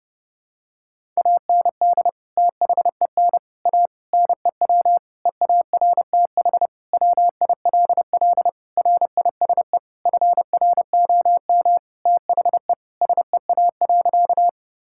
Morse code, heard as a keyed tone, about 30 wpm, 700 Hz. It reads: AND THEN A NEW EARTH WILL RISE FROM THE SEA.
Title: Sentences from Top 1000 Words - Review Part 6 - 30wpm